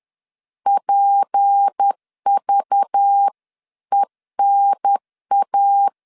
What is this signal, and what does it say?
Morse code, heard as a keyed tone, about 10 wpm, 800 Hz. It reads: PVENA
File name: CW / DET / 1 6 1500C S B 10wpm 4137